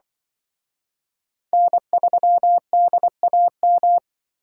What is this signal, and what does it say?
Morse code, heard as a keyed tone, about 24 wpm, 700 Hz. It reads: N3DAM
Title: Speed Racing - USA 1x3 Call Signs 20wpm